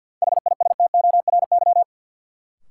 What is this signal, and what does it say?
Morse code, heard as a keyed tone, about 53 wpm, 700 Hz. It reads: HISTORY
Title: Sets of 3 Words - New Word Drill - 50wpm